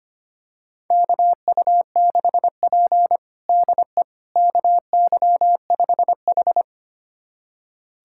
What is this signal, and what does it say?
Morse code, heard as a keyed tone, about 25 wpm, 700 Hz. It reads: KU6P DE KY5H
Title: USA Callsign DE Callsign - 4 Characters - 25wpm